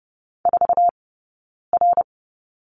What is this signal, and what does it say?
Morse code, heard as a keyed tone, about 30 wpm, spaced as for 9 wpm, 690 Hz. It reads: 4R
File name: course / 30:10wpm / 086 Lesson 15d - Number 5 - Two Characters 30wpm